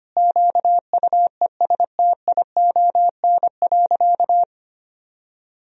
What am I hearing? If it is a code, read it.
Morse code, heard as a keyed tone, about 25 wpm, 700 Hz. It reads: QUESTION.
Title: Sentences from Top 1000 Words - Review Part 2 - 25wpm